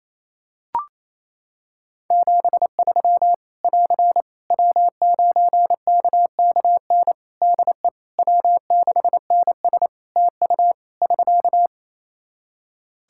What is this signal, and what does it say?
Morse code, heard as a keyed tone, about 28 wpm, 700 Hz. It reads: 73 <AR> W9KKN DE W6NS TU <SK>
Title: Practice QSO - Exchange 4 - Ending - 28wpm